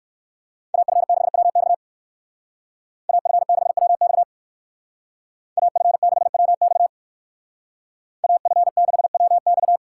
Morse code, heard as a keyed tone, about 58 wpm, 700 Hz. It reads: AF6WX AF6WX AF6WX AF6WX